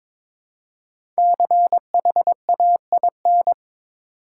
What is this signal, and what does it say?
Morse code, heard as a keyed tone, about 22 wpm, 700 Hz. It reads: CHAIN
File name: 5 Letter Words - First 3 Letters Given - 22wpm